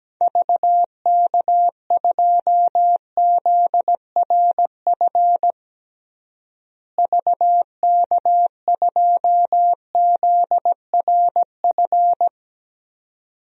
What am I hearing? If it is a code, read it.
Morse code, heard as a keyed tone, about 17 wpm, 700 Hz. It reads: VK2ZRF VK2ZRF
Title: DXCC-100 Call Signs - 6 Characters - Repeated 2x - 17wpm